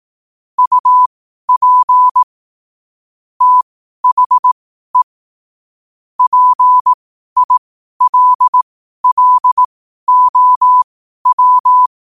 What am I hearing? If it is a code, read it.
Morse code, heard as a keyed tone, about 18 wpm, spaced as for 12 wpm, 1000 Hz. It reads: UP THE PILLOW
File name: qotdfiles / 12wpm